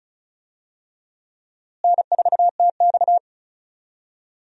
Morse code, heard as a keyed tone, about 36 wpm, 700 Hz. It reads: N4TX
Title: USA Call Signs - 4 Characters - Repeated 3x - 35wpm